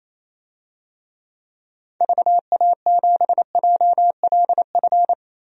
Morse code, heard as a keyed tone, about 28 wpm, 700 Hz. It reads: VA7JLF
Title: DXCC-100 Call Signs - 6 Characters - Repeated 2x - 28wpm